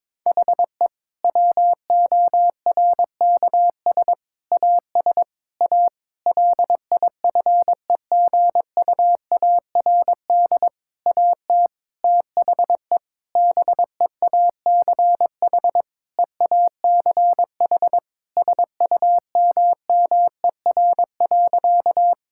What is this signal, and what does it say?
Morse code, heard as a keyed tone, about 22 wpm, 700 Hz. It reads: HE WORKS AS A LIFEGUARD AT THE BEACH EACH SUMMER.
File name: Difficult Sentences with Hints - 22wpm